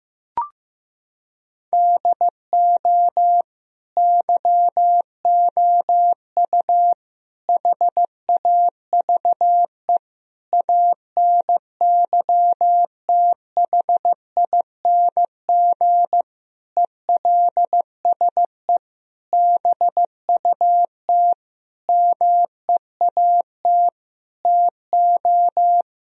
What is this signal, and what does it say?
Morse code, heard as a keyed tone, about 15 wpm, 700 Hz. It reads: DO YOU HAVE ANYTHING ELSE BUT MEAT TO
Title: Sentences from Top 1000 Words - Review Part 3 - 15wpm